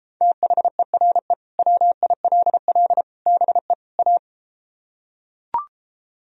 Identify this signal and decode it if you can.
Morse code, heard as a keyed tone, about 33 wpm, 700 Hz. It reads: THERE WILL BE A